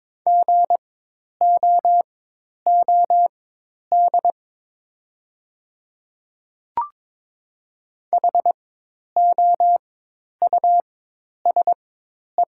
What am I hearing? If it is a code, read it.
Morse code, heard as a keyed tone, about 22 wpm, spaced as for 10 wpm, 700 Hz. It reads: GOOD HOUSE
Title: 4x Character Spacing - Top 300 Words - 22wpm